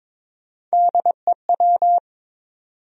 Morse code, heard as a keyed tone, about 22 wpm, 700 Hz. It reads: DEW
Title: USA Call Sign Postfixes - 3 Letters 22wpm